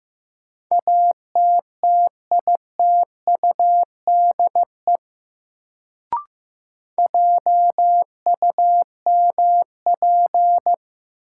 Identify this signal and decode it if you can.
Morse code, heard as a keyed tone, about 15 wpm, 700 Hz. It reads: ATTITUDE JUMP
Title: Sentences from Top 1300 Words - New Word Drill - 15wpm